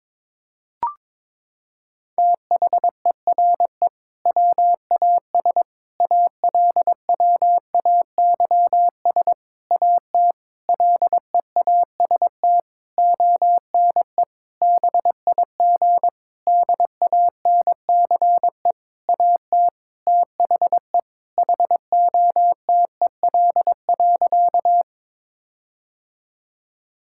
Morse code, heard as a keyed tone, about 22 wpm, 700 Hz. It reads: THERE WAS ALWAYS AT LEAST ONE BIG DANCE AT THE HOTEL.
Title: Sentences from Top 1600 Words 22wpm